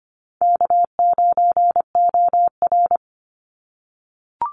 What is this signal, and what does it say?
Morse code, heard as a keyed tone, about 25 wpm, 700 Hz. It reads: K9OR
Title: Call Signs - 25wpm